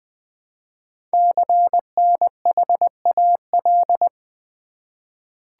Morse code, heard as a keyed tone, about 20 wpm, 700 Hz. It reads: CNHAL
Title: Code Groups - 5 Letters - 20wpm